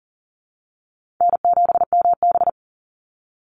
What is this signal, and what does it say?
Morse code, heard as a keyed tone, about 40 wpm, 700 Hz. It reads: N7MB
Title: Easy Call Signs Repeated 3x - 40wpm